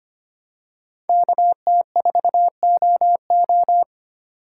Morse code, heard as a keyed tone, about 25 wpm, 700 Hz. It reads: KT4OO
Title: USA Call Signs - 5 Characters - 2x Character Spacing - 25wpm